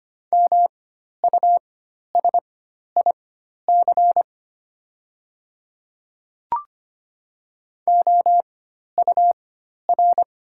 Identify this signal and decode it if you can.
Morse code, heard as a keyed tone, about 25 wpm, spaced as for 12 wpm, 700 Hz. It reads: MUSIC OUR
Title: 4x Character Spacing - Top 300 Words - 25wpm